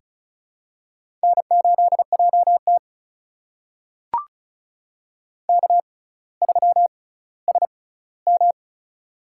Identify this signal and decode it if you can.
Morse code, heard as a keyed tone, about 36 wpm, spaced as for 18 wpm, 700 Hz. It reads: N8JT K3SM
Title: USA Call Signs - 4 Characters - 6x Character Spacing - 35wpm